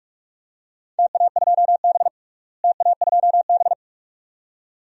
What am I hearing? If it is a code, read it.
Morse code, heard as a keyed tone, about 44 wpm, 700 Hz. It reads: TA2B TA2B
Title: DXCC-100 Call Signs - 4 Characters - Repeated 2x - 45wpm